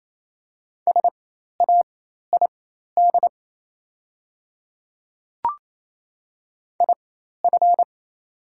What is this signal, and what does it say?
Morse code, heard as a keyed tone, about 27 wpm, spaced as for 13 wpm, 690 Hz. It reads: SAID IF
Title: 4x Character Spacing - Top 100 Words - 28wpm